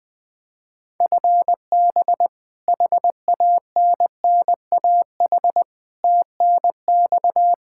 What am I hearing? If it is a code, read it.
Morse code, heard as a keyed tone, about 20 wpm, 700 Hz. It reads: FB HANNAH TNX
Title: Practice QSO - Exchange 3 - More Info - 20wpm